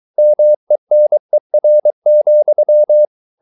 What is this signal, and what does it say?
Morse code, heard as a keyed tone, about 23 wpm, 600 Hz. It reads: MENER,
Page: Ferie 23 wpm